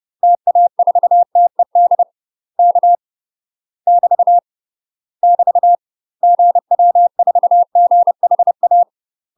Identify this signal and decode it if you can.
Morse code, heard as a keyed tone, about 30 wpm, 700 Hz. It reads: TA4TED K <BT> <BT> GW4GHA
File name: UK MASTER QSOs TRAINING / 30WPM / UK MASTER QSOs 0017